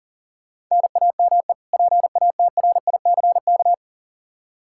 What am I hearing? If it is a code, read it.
Morse code, heard as a keyed tone, about 37 wpm, 700 Hz. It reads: NAME PATRICK
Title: Practice QSO - Exchange 1 - Beginning - 40wpm